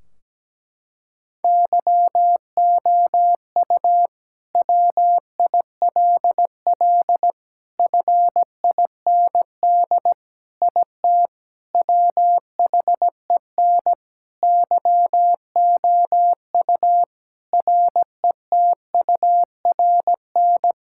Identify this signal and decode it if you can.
Morse code, heard as a keyed tone, about 17 wpm, 700 Hz. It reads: YOU WILL FIND IT WHEN YOU RETURN
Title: Sentences from Top 1600 Words 17wpm